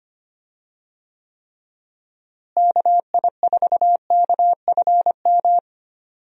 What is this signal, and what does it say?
Morse code, heard as a keyed tone, about 25 wpm, 700 Hz. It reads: KI4KFM
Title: USA Call Signs - 6 Characters - Repeated 3x - 25wpm